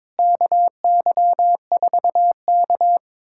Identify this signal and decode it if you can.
Morse code, heard as a keyed tone, about 22 wpm, 700 Hz. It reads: KY4K
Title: USA Callsign DE Callsign - 4 Characters - 22wpm